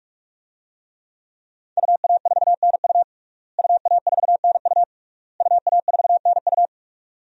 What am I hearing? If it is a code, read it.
Morse code, heard as a keyed tone, about 46 wpm, 700 Hz. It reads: UA4NU UA4NU UA4NU
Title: DXCC-100 Call Signs - 5 Characters - Repeated 3x - 45wpm